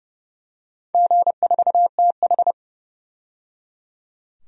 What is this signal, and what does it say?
Morse code, heard as a keyed tone, about 29 wpm, 700 Hz. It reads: G4TH
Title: DXCC-100 Call Signs - 4 Characters - 30wpm